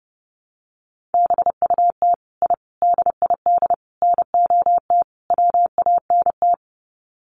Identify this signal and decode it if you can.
Morse code, heard as a keyed tone, about 30 wpm, 700 Hz. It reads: BUT I DID NOT WANT
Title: Sets of 5 Words Encore 30wpm